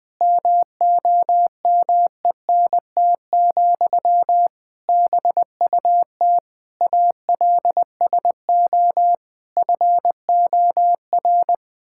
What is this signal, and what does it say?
Morse code, heard as a keyed tone, about 20 wpm, 700 Hz. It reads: MOMENT, BUT ALSO FOR
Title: Sentences from Top 1000 Words - Review Part 6 - 20wpm